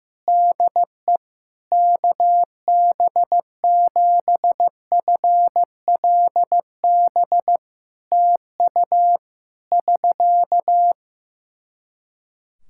Morse code, heard as a keyed tone, about 15 wpm, 700 Hz. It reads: DE KB7FLB TU <SK>
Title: Practice QSO - Exchange 4 - Ending - 15wpm